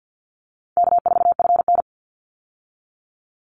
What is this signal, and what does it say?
Morse code, heard as a keyed tone, about 48 wpm, 700 Hz. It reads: K4FN
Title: Easy Call Signs Repeated 2x - 50wpm